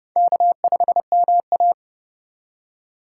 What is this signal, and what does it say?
Morse code, heard as a keyed tone, about 30 wpm, 700 Hz. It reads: K5MA
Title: K1USN SST QSO Practice - 30wpm